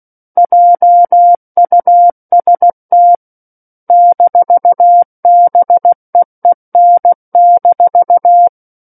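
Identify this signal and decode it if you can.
Morse code, heard as a keyed tone, about 16 wpm, 700 Hz. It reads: JUST -BEEN-
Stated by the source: Morse practice of book text at 16 wpm.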